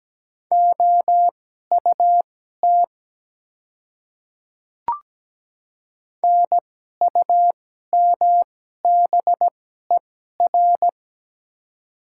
Morse code, heard as a keyed tone, about 17 wpm, spaced as for 12 wpm, 700 Hz. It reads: OUT NUMBER
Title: 2x Character Spacing - Top 100 Words - 17wpm